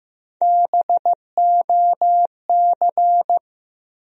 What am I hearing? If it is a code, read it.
Morse code, heard as a keyed tone, about 15 wpm, 700 Hz. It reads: BOC
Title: USA Call Sign Postfixes - 3 Letters 15wpm